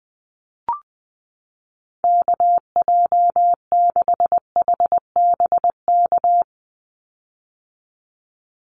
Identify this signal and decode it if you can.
Morse code, heard as a keyed tone, about 20 wpm, 700 Hz. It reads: KJ6HBK
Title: USA Call Signs 20wpm